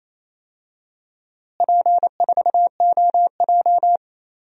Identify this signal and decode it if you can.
Morse code, heard as a keyed tone, about 28 wpm, 700 Hz. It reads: P4OJ
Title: DXCC-100 Call Signs - 4 Characters - Repeated 3x - 28wpm